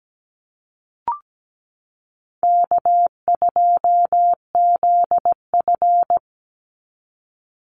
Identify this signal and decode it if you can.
Morse code, heard as a keyed tone, about 17 wpm, 700 Hz. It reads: K2ZF
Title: Easy Call Signs - 17wpm